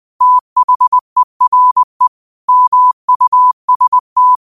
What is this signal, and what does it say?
Morse code, heard as a keyed tone, about 20 wpm, 1000 Hz. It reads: THERE MUST